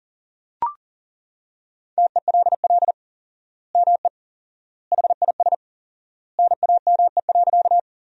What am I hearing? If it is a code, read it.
Morse code, heard as a keyed tone, about 37 wpm, 700 Hz. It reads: TELL ME HIS NAME.
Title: 4x Word-Spacing - Sentences from Top 200 Words - 40wpm